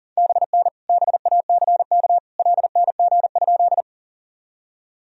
Morse code, heard as a keyed tone, about 40 wpm, 700 Hz. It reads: BN BACK LNG?
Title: Sentences from Top 200 Words and Abbreviations 40wpm